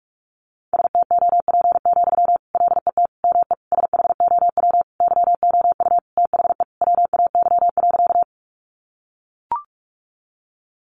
Morse code, heard as a keyed tone, about 44 wpm, 700 Hz. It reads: STOP, LET ME SHOW YOU THE WAY.
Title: Sentences from Top 300 Words 45wpm